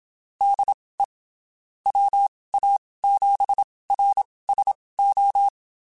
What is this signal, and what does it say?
Morse code, heard as a keyed tone, about 27 wpm, spaced as for 19 wpm, 800 Hz. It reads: DE WA7RSO
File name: (spaced) / QSOs Spaced-mp3s / kc7haa @25 sp